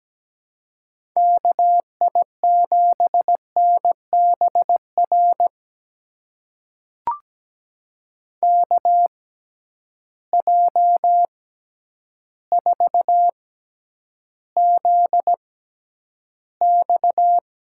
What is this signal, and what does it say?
Morse code, heard as a keyed tone, about 17 wpm, spaced as for 9 wpm, 700 Hz. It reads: KI7NBR KJ4ZX
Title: USA Call Signs - 6x Character Spacing - 17wpm